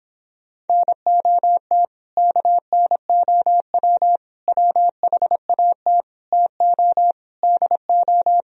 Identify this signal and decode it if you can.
Morse code, heard as a keyed tone, about 26 wpm, 700 Hz. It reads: NOT KNOW WHAT TO DO